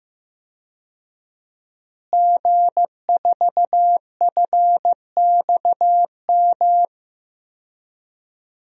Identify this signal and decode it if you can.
Morse code, heard as a keyed tone, about 15 wpm, 700 Hz. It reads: G4FXM
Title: DXCC-100 Call Signs - 5 Characters - Repeated 2x - 15wpm